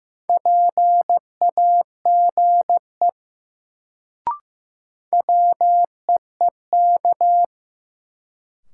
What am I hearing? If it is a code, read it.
Morse code, heard as a keyed tone, about 15 wpm, 700 Hz. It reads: PAGE WEEK
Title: Sets of 5 Words Encore - New Word Drill - 15wpm